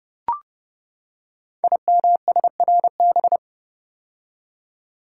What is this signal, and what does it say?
Morse code, heard as a keyed tone, about 28 wpm, 700 Hz. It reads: IMSRB
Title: Code Groups - 5 Letters - 30wpm